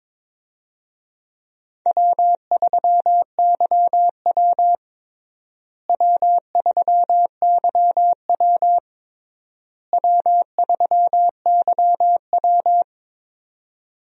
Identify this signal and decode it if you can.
Morse code, heard as a keyed tone, about 22 wpm, 700 Hz. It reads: W3YW W3YW W3YW
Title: USA Call Signs - 4 Characters - Repeated 3x - 22wpm